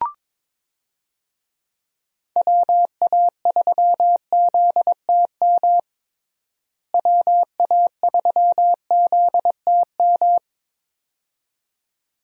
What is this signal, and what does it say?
Morse code, heard as a keyed tone, about 22 wpm, 700 Hz. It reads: WA3ZTM WA3ZTM
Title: USA Call Signs - 6 Characters - Repeated 2x - 22wpm